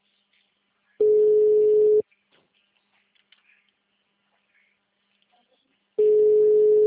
Ringback tone (Europe, 425 Hz).